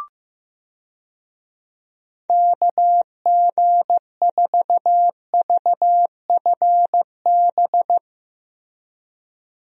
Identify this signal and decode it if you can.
Morse code, heard as a keyed tone, about 15 wpm, 700 Hz. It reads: KG4VFB
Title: USA Call Signs - 6 Characters - Repeated 2x - 15wpm